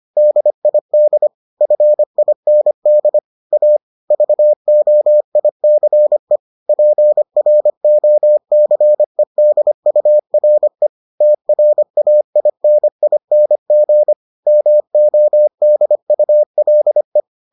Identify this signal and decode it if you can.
Morse code, heard as a keyed tone, about 25 wpm, 600 Hz. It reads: DID FIND A VOICE PROCEDURE TRAINING MODULE